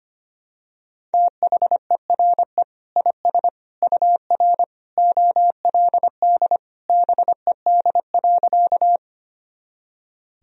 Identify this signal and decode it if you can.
Morse code, heard as a keyed tone, about 25 wpm, 700 Hz. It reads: THERE IS UR OLD BED.
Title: Sentences from Top 500 Words and Abbreviations 25wpm